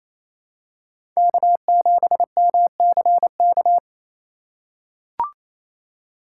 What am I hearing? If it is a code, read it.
Morse code, heard as a keyed tone, about 28 wpm, 690 Hz. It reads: K7MCK
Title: USA Call Signs - 5 Characters - 4x Character Spacing - 28wpm